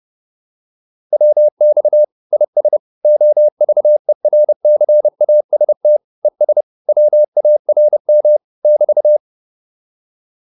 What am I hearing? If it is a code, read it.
Morse code, heard as a keyed tone, about 30 wpm, 600 Hz. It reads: WX IS OVERCAST ES WARM <BT>